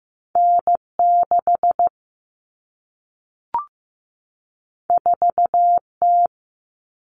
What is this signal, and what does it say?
Morse code, heard as a keyed tone, about 15 wpm, 700 Hz. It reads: N6 4T
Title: All Cut Numbers - 2-Digit 15wpm